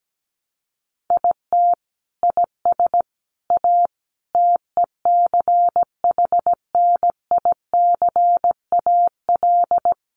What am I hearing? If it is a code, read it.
Morse code, heard as a keyed tone, about 17 wpm, 700 Hz. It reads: IT IS A TECHNICAL
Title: Sentences from Top 1100 Words 17wpm